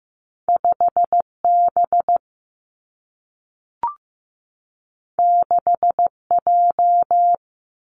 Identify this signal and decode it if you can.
Morse code, heard as a keyed tone, about 15 wpm, 700 Hz. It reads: 5 B 6 J
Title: Two Challenging Letter-Number Combinations 15wpm